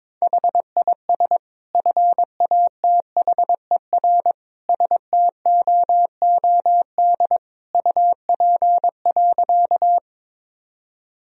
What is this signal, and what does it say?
Morse code, heard as a keyed tone, about 22 wpm, 700 Hz. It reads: HIS FATHER STOOD UP.
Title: Sentences from Top 1000 Words - Review Part 2 - 22wpm